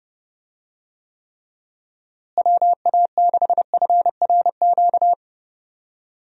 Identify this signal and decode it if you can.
Morse code, heard as a keyed tone, about 32 wpm, 700 Hz. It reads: WA6FRQ
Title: USA Call Signs - 6 Characters - Repeated 2x - 30wpm